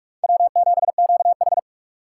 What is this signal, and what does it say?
Morse code, heard as a keyed tone, about 46 wpm, 700 Hz. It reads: W7QH
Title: USA Call Signs - 4 Characters - 6x Character Spacing - 45wpm